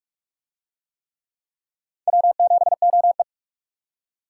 Morse code, heard as a keyed tone, about 41 wpm, 700 Hz. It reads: W7OE